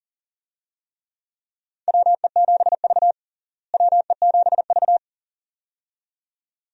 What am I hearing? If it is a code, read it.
Morse code, heard as a keyed tone, about 37 wpm, 700 Hz. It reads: WE7V WE7V
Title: USA Call Signs - 4 Characters - Repeated 2x - 40wpm